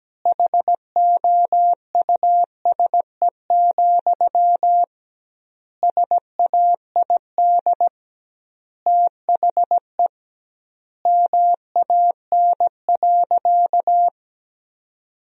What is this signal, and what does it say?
Morse code, heard as a keyed tone, about 17 wpm, 700 Hz. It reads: HOUSE, SAID THE MAN.
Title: 2x Word-Spacing - Sentences from Top 400 Words - 17wpm